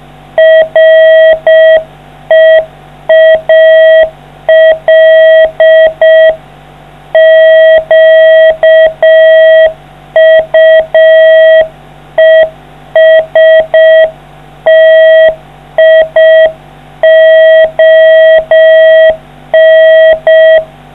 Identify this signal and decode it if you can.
Morse code, heard as a keyed tone, about 6 wpm, 660 Hz. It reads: REAL QUESTION